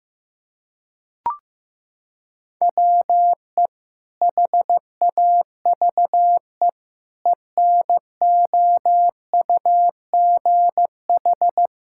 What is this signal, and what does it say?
Morse code, heard as a keyed tone, about 15 wpm, 700 Hz. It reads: WE HAVE ENOUGH
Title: Sentences from Top 1000 Words - Review Part 5 - 15wpm